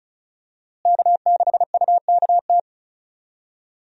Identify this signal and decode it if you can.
Morse code, heard as a keyed tone, about 36 wpm, 700 Hz. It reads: K6UKT
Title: USA Call Signs - 5 Characters - Repeated 3x - 35wpm